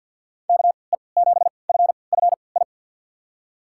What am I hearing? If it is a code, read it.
Morse code, heard as a keyed tone, about 49 wpm, spaced as for 28 wpm, 700 Hz. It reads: KE7FFI
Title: USA Call Signs - 6 Characters - 3x Character Spacing - 50wpm